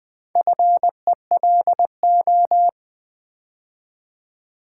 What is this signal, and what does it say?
Morse code, heard as a keyed tone, about 20 wpm, 700 Hz. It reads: FELO
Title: Build A Word - 2 Letters At A Time - 6-Letter Words - 20wpm